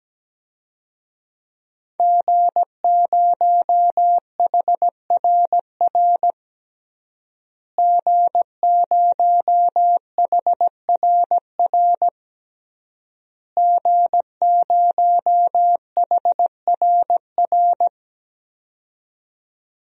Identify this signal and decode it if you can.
Morse code, heard as a keyed tone, about 17 wpm, 700 Hz. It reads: G0HRR G0HRR G0HRR
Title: DXCC-100 Call Signs - 5 Characters - Repeated 3x - 17wpm